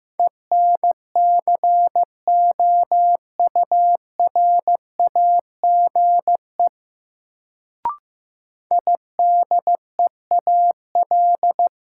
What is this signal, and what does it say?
Morse code, heard as a keyed tone, about 15 wpm, 700 Hz. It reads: ENCOURAGE IDEAL
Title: Sentences from Top 1400 Words - New Word Drill Alt - 15wpm